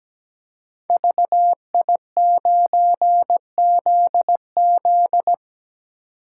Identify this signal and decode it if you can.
Morse code, heard as a keyed tone, about 17 wpm, 700 Hz. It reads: VI9ZZ
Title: Code Groups - 5 Letter:Numbers - 17wpm